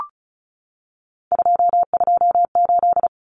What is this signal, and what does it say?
Morse code, heard as a keyed tone, about 34 wpm, 700 Hz. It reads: 228
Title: Common Cut Numbers - 3-Digit 35wpm